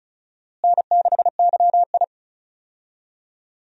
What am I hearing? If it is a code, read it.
Morse code, heard as a keyed tone, about 34 wpm, 700 Hz. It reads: N6YI
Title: USA Call Signs - 4 Characters - Repeated 2x - 35wpm